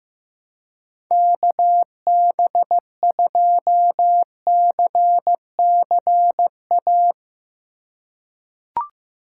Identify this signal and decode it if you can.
Morse code, heard as a keyed tone, about 15 wpm, 700 Hz. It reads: KB2CCA
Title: USA Call Signs - 6 Characters - 3x Character Spacing - 15wpm